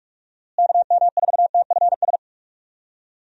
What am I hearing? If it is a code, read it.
Morse code, heard as a keyed tone, about 46 wpm, 700 Hz. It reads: KM4TFS
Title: USA Call Signs - 6 Characters - Repeated 3x - 45wpm